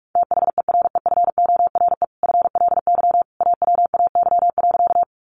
Morse code, heard as a keyed tone, about 44 wpm, 700 Hz. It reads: THEREFORE FLY AWAY.
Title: Sentences from Top 800 Words 45wpm